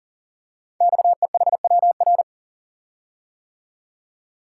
Morse code, heard as a keyed tone, about 40 wpm, 700 Hz. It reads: XEHWR
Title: Code Groups - 5 Letters - 40wpm